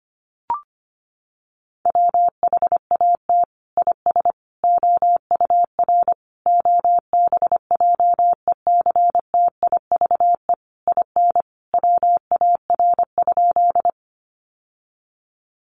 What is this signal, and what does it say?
Morse code, heard as a keyed tone, about 25 wpm, 700 Hz. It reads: WHAT IS OUR OBJECTIVE IN WAR?
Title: Sentences from Top 1200 Words 25wpm